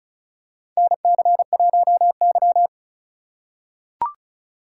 Morse code, heard as a keyed tone, about 36 wpm, 700 Hz. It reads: NC1Y
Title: USA Call Signs - 4 Characters - 3x Character Spacing - 35wpm